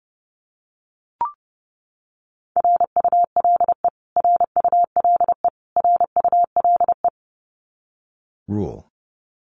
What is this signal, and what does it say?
Morse code, heard as a keyed tone, about 28 wpm, 700 Hz. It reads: RULE RULE RULE
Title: Top 500 Words Repeated 30wpm